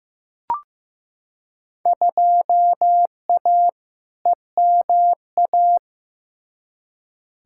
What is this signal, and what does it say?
Morse code, heard as a keyed tone, about 15 wpm, 700 Hz. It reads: 2A EMA
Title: Field Day Exchanges Encore - 15wpm